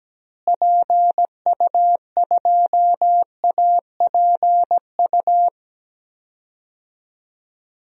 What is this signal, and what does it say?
Morse code, heard as a keyed tone, about 17 wpm, 700 Hz. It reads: PU2APU
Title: DXCC-100 Call Signs - 6 Characters - Repeated 2x - 17wpm